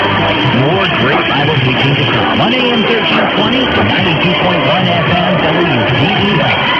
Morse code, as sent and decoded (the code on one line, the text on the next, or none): none